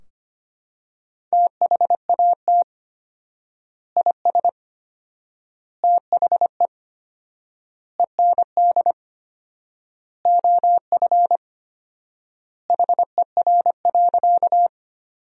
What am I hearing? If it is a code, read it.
Morse code, heard as a keyed tone, about 25 wpm, 700 Hz. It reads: THAT IS THE END OF HER.
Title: 4x Word-Spacing - Sentences from Top 200 Words - 25wpm